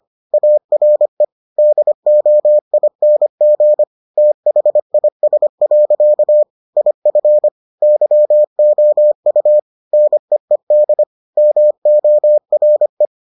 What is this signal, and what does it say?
Morse code, heard as a keyed tone, about 25 wpm, 600 Hz. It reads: ARE DOING THIS. IF YOU NEED MORE